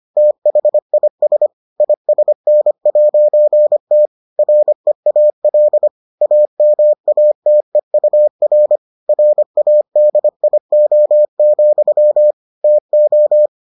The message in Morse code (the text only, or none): THIS ISN'T REAL AMATEUR RADIO, TO